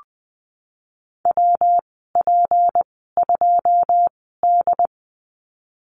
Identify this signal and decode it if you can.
Morse code, heard as a keyed tone, about 20 wpm, spaced as for 15 wpm, 700 Hz. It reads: WP2D
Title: 173 Lesson 28g - Number 9 - Call Signs 20wpm